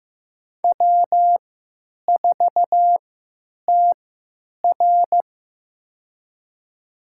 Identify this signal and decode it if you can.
Morse code, heard as a keyed tone, about 15 wpm, spaced as for 9 wpm, 700 Hz. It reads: W4TR
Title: USA Call Signs - 4 Characters - 3x Character Spacing - 15wpm